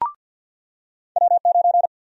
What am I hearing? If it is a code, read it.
Morse code, heard as a keyed tone, about 53 wpm, 700 Hz. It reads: W9
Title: USA Call Sign Prefixes - 50wpm